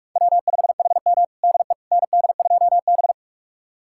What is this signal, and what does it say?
Morse code, heard as a keyed tone, about 46 wpm, 700 Hz. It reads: W5HM DE ND2B
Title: USA Callsign DE Callsign - 4 Characters - 45wpm